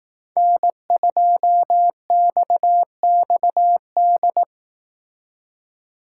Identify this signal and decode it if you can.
Morse code, heard as a keyed tone, about 18 wpm, 700 Hz. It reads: N2XXD
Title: Speed Racing - USA 1x3 Call Signs 15wpm